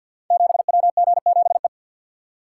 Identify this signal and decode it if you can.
Morse code, heard as a keyed tone, about 46 wpm, 700 Hz. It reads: 7WG7E
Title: Code Groups - 5 Letter:Numbers - 50wpm